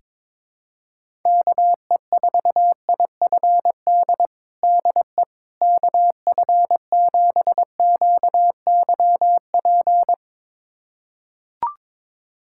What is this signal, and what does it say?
Morse code, heard as a keyed tone, about 22 wpm, 700 Hz. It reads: KE4IFD DE KF7QYP
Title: USA Callsign DE Callsign - 6 Characters - 22wpm